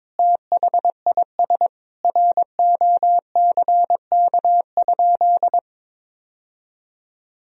Morse code, heard as a keyed tone, about 21 wpm, 700 Hz. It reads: THIS ROCK?
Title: Sentences from Top 1000 Words - Review Part 2 - 22wpm